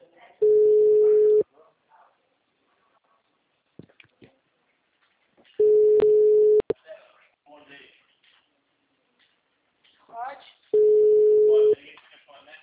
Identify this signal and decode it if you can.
Ringback tone (Europe, 425 Hz), with speech.
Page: TLML PRUDENTE